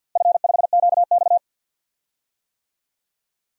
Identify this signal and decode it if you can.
Morse code, heard as a keyed tone, about 52 wpm, 700 Hz. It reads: F5QX
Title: DXCC-100 Call Signs - 4 Characters - Repeated 2x - 50wpm